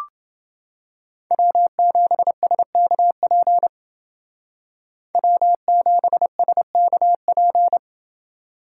Morse code, heard as a keyed tone, about 29 wpm, 700 Hz. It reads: W7SKP W7SKP